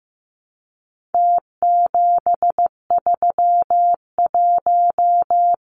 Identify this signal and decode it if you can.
Morse code, heard as a keyed tone, about 15 wpm, 700 Hz. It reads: T731
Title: Numbers - Contest Style - 4-Digit 15wpm